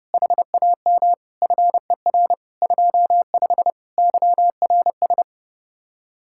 Morse code, heard as a keyed tone, about 30 wpm, 700 Hz. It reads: HAM FER 25 YRS